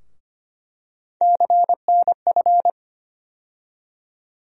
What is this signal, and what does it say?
Morse code, heard as a keyed tone, about 25 wpm, 700 Hz. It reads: CNF
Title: USA Call Sign Postfixes - 3 Letters 25wpm